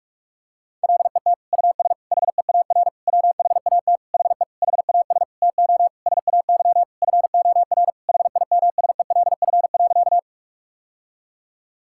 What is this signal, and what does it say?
Morse code, heard as a keyed tone, about 43 wpm, 700 Hz. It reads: LET US HEAR WHAT HE HAS TO SAY FOR HIMSELF.
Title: Sentences from Top 1600 Words 45wpm